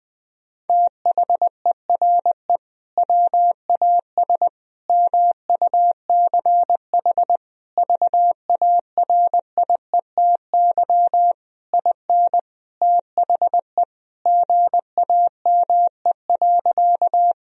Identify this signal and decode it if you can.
Morse code, heard as a keyed tone, about 20 wpm, 700 Hz. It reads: THERE WAS MUCH VARIETY IN THE GAME.
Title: Sentences from Top 1000 Words - Review Part 4 - 20wpm